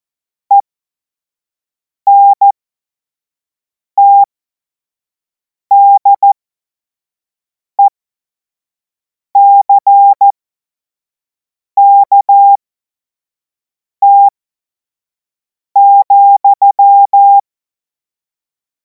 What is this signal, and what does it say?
Morse code, heard as a keyed tone, about 14 wpm, spaced as for 5 wpm, 800 Hz. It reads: ENTDECKT,
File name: Morsekurs / Witze / Witz 24. Februar 2026 25 F70